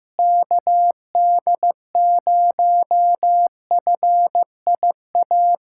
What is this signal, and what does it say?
Morse code, heard as a keyed tone, about 15 wpm, 700 Hz. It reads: KD0FIA
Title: USA Callsign DE Callsign 15wpm